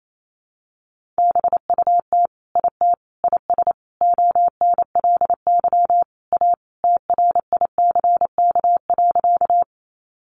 Morse code, heard as a keyed tone, about 28 wpm, 700 Hz. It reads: BUT IT IS ONLY A TRICK.
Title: Sentences from Top 1500 Words 28wpm